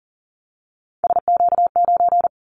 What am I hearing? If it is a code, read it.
Morse code, heard as a keyed tone, about 38 wpm, 700 Hz. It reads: SQ9
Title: Callsign Prefixes for Top 25 DXCC - 40wpm